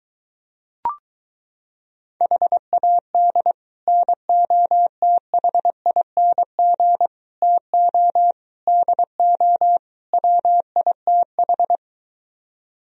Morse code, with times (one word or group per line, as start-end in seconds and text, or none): 2.2-3.5 HAD
3.9-7.1 NOTHING
7.4-8.3 TO
8.7-9.8 DO
10.1-11.8 WITH